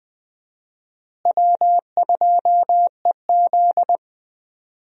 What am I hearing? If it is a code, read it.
Morse code, heard as a keyed tone, about 20 wpm, 700 Hz. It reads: W2EZ
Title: USA Call Signs - 4 Characters - Repeated 3x - 20wpm